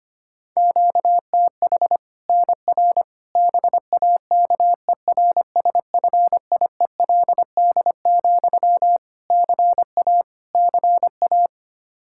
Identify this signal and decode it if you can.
Morse code, heard as a keyed tone, about 25 wpm, 700 Hz. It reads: QTH NR BAKERSFIELD, CA CA
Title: Practice QSO - Exchange 1 - Beginning - 25wpm